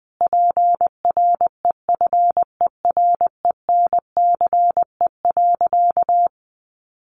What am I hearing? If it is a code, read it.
Morse code, heard as a keyed tone, about 20 wpm, 700 Hz. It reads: PREFERENCE.
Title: Sentences from Top 1500 Words 20wpm